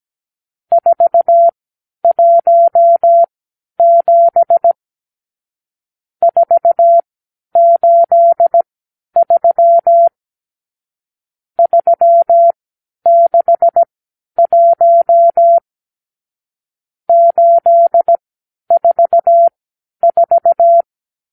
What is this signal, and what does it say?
Morse code, heard as a keyed tone, about 17 wpm, spaced as for 10 wpm, 680 Hz. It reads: 417 483 361 844
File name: arrl / Review (1-0)